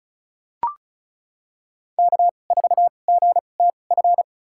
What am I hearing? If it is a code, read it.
Morse code, heard as a keyed tone, about 34 wpm, spaced as for 25 wpm, 700 Hz. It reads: K4GTF
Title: USA Call Signs - 2x Character Spacing - 35wpm